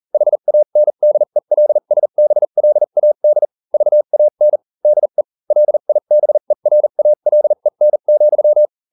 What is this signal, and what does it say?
Morse code, heard as a keyed tone, about 40 wpm, spaced as for 35 wpm, 600 Hz. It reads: HANDELSBLAD VAN DE LIBERALEN,